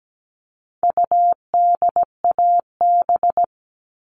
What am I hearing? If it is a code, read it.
Morse code, heard as a keyed tone, about 17 wpm, 700 Hz. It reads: UDAB
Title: All Cut Numbers - 4-Digit 17wpm